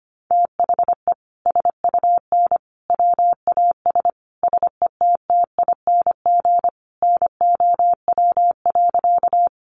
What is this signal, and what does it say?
Morse code, heard as a keyed tone, about 25 wpm, 700 Hz. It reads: THE SUN WAS SETTING NOW.
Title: Sentences from Top 1300 Words 25wpm